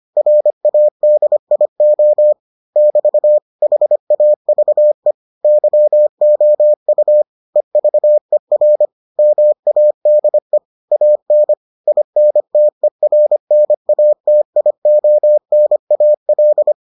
Morse code, heard as a keyed tone, about 25 wpm, 600 Hz. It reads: RADIO <BT> HAVE YOU EVER MADE AN INTERNATIONAL